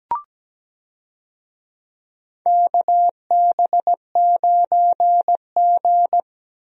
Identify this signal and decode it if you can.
Morse code, heard as a keyed tone, about 17 wpm, 700 Hz. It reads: KB9G
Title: USA Call Signs - 4 Characters - Repeated 2x - 17wpm